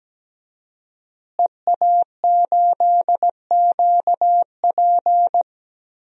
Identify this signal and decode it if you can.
Morse code, heard as a keyed tone, about 17 wpm, 700 Hz. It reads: EA8QP
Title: DXCC-100 Call Signs - 5 Characters - Repeated 3x - 17wpm